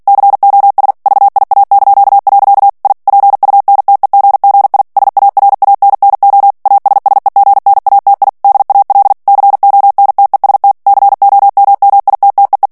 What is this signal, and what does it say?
Morse code, heard as a keyed tone, about 48 wpm, 800 Hz. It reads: COS VIA). I PUNTEGGI SARANNO ASSEGNATI DAL CONTEST COMMITTEE
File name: ITALIEN texte clair 2 vitesse 45 mots minute